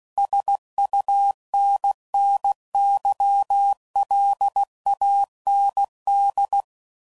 Morse code, heard as a keyed tone, about 16 wpm, 800 Hz. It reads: SUNNYLAND
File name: CW Code Groups @15WPM / fromSM's / Cities